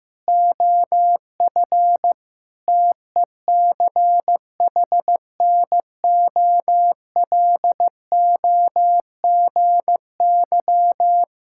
Morse code, heard as a keyed tone, about 15 wpm, 700 Hz. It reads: OF TECHNOLOGY
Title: Sentences from Top 1000 Words - Review Part 5 - 15wpm